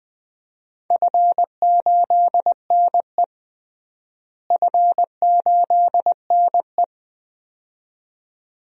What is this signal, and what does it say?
Morse code, heard as a keyed tone, about 21 wpm, 700 Hz. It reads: F8NE F8NE
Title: DXCC-100 Call Signs - 4 Characters - Repeated 2x - 20wpm